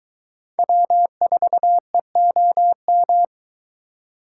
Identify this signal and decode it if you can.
Morse code, heard as a keyed tone, about 22 wpm, 700 Hz. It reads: W4EOM